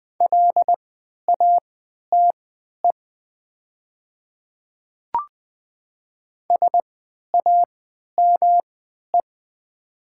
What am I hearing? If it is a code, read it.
Morse code, heard as a keyed tone, about 20 wpm, spaced as for 11 wpm, 700 Hz. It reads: LATE SAME
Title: 3x Character Spacing - Top 500 Words - 20wpm